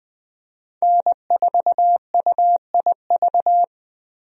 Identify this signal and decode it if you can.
Morse code, heard as a keyed tone, about 20 wpm, 700 Hz. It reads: N4UIV
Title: USA Call Signs - 5 Characters - 20wpm